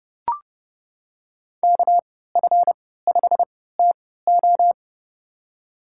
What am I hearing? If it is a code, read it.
Morse code, heard as a keyed tone, about 31 wpm, spaced as for 17 wpm, 700 Hz. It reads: KF5TO
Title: USA Call Signs - 5 Characters - 3x Character Spacing - 30wpm